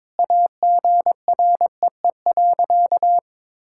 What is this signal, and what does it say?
Morse code, heard as a keyed tone, about 22 wpm, 700 Hz. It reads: AGREE.
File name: practice / 22wpm / Sentences with 8-Word Phrases - 22wpm